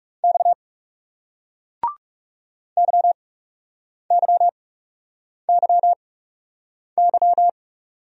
Morse code, heard as a keyed tone, about 39 wpm, spaced as for 8 wpm, 700 Hz. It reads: X YYYY